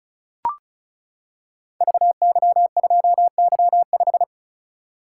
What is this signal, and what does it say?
Morse code, heard as a keyed tone, about 36 wpm, 700 Hz. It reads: VY2Y5
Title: Code Groups - 5 Letter:Numbers - 35wpm